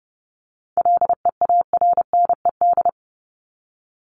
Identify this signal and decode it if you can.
Morse code, heard as a keyed tone, about 30 wpm, 700 Hz. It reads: LEARNED
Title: ED Suffix - Top 100 Words with Base Words 30wpm